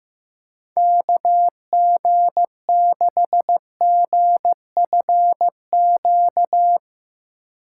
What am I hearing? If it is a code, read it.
Morse code, heard as a keyed tone, about 15 wpm, 700 Hz. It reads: KG6GFQ
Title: USA Call Signs - 6 Characters - 2x Character Spacing - 15wpm